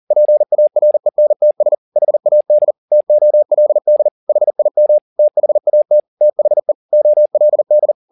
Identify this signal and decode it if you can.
Morse code, heard as a keyed tone, about 40 wpm, 600 Hz. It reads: PARENTS HAD TOLD HIM THAT THE OLD
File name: the old man and the sea 1 40wpm